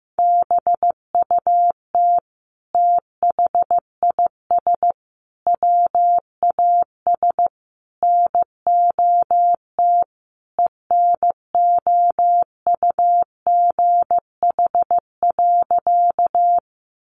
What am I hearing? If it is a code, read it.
Morse code, heard as a keyed tone, about 15 wpm, 700 Hz. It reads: BUT THIS WAS NOT ENOUGH.
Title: Sentences from Top 600 Words 15wpm